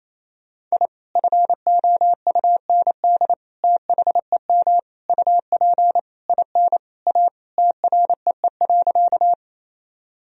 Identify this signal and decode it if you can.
Morse code, heard as a keyed tone, about 28 wpm, 700 Hz. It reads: I FOUND THEM UP IN A TREE.
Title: Sentences from Top 1000 Words - Review Part 1 - 28wpm